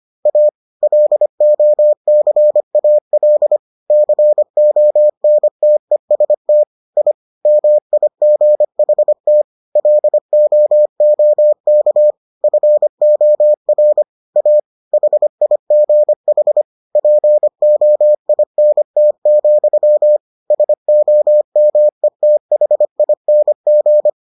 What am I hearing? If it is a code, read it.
Morse code, heard as a keyed tone, about 25 wpm, 600 Hz. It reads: A LOCAL CONTEST I MIGHT LOOK FOR A HIGH POINT, SOMETHING